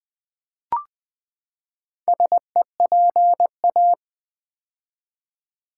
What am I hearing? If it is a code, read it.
Morse code, heard as a keyed tone, about 20 wpm, 700 Hz. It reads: SEPA
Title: Build A Word - 4 Letters At A Time - 8-Letter Words - 20wpm